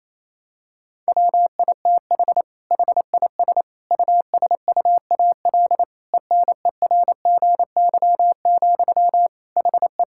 Morse code, heard as a keyed tone, about 27 wpm, 700 Hz. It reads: WITH HIS USUAL ENERGY, HE
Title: Sentences from Top 1000 Words - Review Part 6 - 28wpm